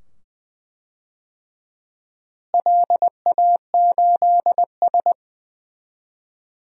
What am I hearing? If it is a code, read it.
Morse code, heard as a keyed tone, about 21 wpm, 700 Hz. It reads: LA8S
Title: DXCC-100 Call Signs - 4 Characters - Repeated 2x - 20wpm